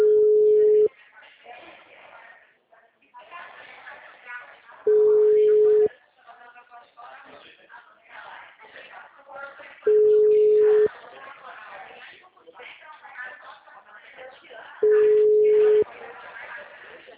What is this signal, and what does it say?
Ringback tone (Europe, 425 Hz), with speech.